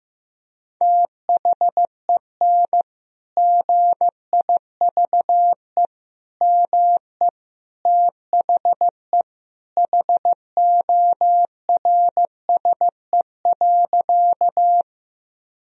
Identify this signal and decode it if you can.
Morse code, heard as a keyed tone, about 15 wpm, 700 Hz. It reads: THEN GIVE ME THE HORSE.
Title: Sentences from Top 1000 Words - Review Part 1 - 15wpm